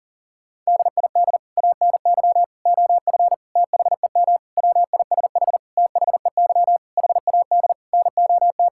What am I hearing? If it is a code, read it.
Morse code, heard as a keyed tone, about 40 wpm, 700 Hz. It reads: DID ANY OF THEM WISH THEY HAD NOT
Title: Sentences from Top 1600 Words 40wpm